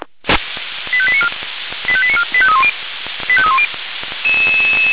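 Five-tone selective call (ZVEI), heard as quick runs of five tones, tones of about 70 ms. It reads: 85894 85894 85310 85310
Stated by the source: Received on an SDR; ZVEI five-tone call 85894.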